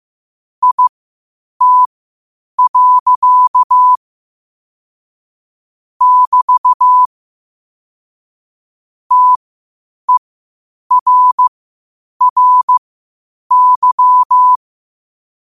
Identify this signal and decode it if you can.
Morse code, heard as a keyed tone, about 15 wpm, spaced as for 8 wpm, 1000 Hz. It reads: IT. <BT> TERRY
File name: qotdfiles / 8wpm